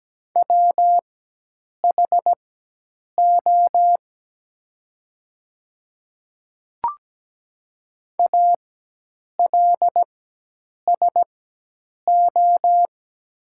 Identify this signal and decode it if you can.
Morse code, heard as a keyed tone, about 17 wpm, spaced as for 8 wpm, 690 Hz. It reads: WHO ALSO
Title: 4x Character Spacing - Top 300 Words - 17wpm